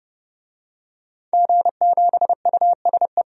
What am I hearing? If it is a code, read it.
Morse code, heard as a keyed tone, about 30 wpm, 700 Hz. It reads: G7USE
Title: DXCC-100 Call Signs - 5 Characters - 30wpm